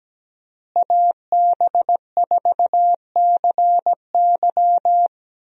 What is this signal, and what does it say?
Morse code, heard as a keyed tone, about 17 wpm, 700 Hz. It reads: AB4CY
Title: USA Call Signs - 5 Characters - Repeated 2x - 17wpm